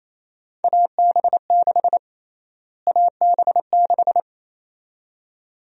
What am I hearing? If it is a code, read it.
Morse code, heard as a keyed tone, about 27 wpm, 700 Hz. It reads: AB6 AB6
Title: USA Call Sign Prefixes - Repeated 2x - 28wpm